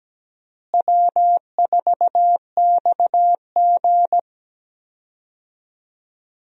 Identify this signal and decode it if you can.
Morse code, heard as a keyed tone, about 17 wpm, 700 Hz. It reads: W4XG
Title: USA Call Signs - 4 Characters - Repeated 2x - 17wpm